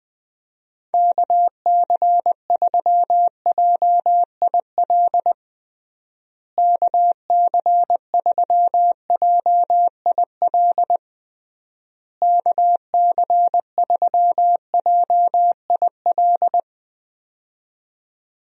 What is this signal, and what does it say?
Morse code, heard as a keyed tone, about 20 wpm, 700 Hz. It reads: KC3JIL KC3JIL KC3JIL
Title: USA Call Signs - 6 Characters - Repeated 3x - 20wpm